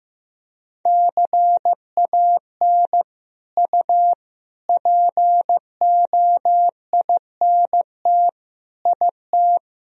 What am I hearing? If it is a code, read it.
Morse code, heard as a keyed tone, about 15 wpm, 700 Hz. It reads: CAN U POINT IT
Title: Sentences from Top 200 Words and Abbreviations 15wpm